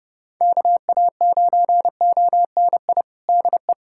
Morse code, heard as a keyed tone, about 30 wpm, 700 Hz. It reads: KA9ONI DE
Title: USA Callsign DE Callsign - 6 Characters - 30wpm